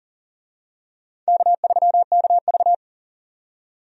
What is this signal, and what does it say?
Morse code, heard as a keyed tone, about 40 wpm, 700 Hz. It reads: K3KV
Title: USA Call Signs - 4 Characters - 40wpm